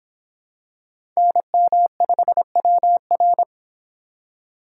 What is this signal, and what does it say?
Morse code, heard as a keyed tone, about 26 wpm, 700 Hz. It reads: NM5WR